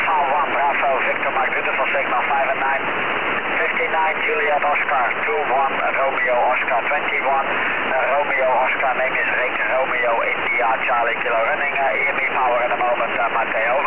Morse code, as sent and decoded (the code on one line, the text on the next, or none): none